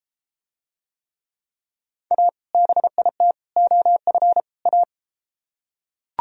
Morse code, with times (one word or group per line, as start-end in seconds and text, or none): 2.1-2.3 A
2.5-3.3 BIT
3.6-4.4 OF
4.7-4.8 A